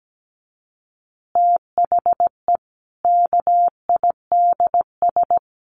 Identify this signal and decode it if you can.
Morse code, heard as a keyed tone, about 17 wpm, 700 Hz. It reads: THE KIDS
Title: Easy Phrases 17wpm